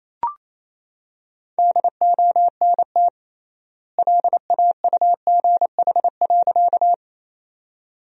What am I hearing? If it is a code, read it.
Morse code, heard as a keyed tone, about 28 wpm, 700 Hz. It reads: DONT LAUGH.
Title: 3x Word-Spacing - Sentences from Top 500 Words - 28wpm